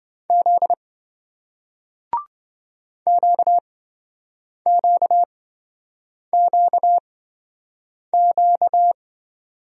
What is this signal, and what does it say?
Morse code, heard as a keyed tone, about 27 wpm, spaced as for 7 wpm, 700 Hz. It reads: Z QQQQ